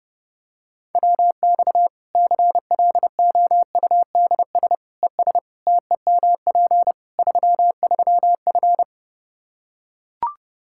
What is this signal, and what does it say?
Morse code, heard as a keyed tone, about 30 wpm, 700 Hz. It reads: WX CLOUDS ES TEMP 33F
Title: K1USN SST QSO Practice 30wpm